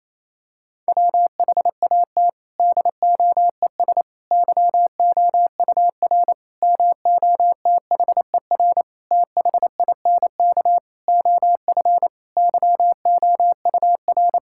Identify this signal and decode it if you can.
Morse code, heard as a keyed tone, about 28 wpm, 700 Hz. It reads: WHAT DOES YOUR MOTHER THINK OF YOUR
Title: Sentences from Top 1000 Words - Review Part 5 - 28wpm